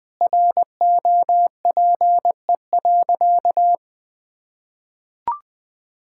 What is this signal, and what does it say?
Morse code, heard as a keyed tone, about 20 wpm, 700 Hz. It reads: ROPE.
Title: Easy Sentences with Hints - 20wpm